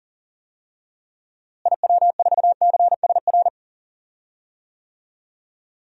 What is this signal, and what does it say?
Morse code, heard as a keyed tone, about 40 wpm, 700 Hz. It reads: IW4CSR
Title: DXCC-100 Call Signs - 6 Characters - Repeated 3x - 40wpm